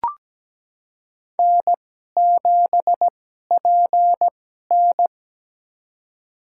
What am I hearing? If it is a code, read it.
Morse code, heard as a keyed tone, about 17 wpm, spaced as for 12 wpm, 700 Hz. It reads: N7PN
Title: USA Call Signs - 4 Characters - 2x Character Spacing - 17wpm